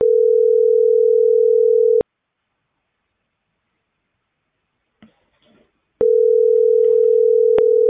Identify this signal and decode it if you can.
Ringback tone (North America), 440 + 480 Hz.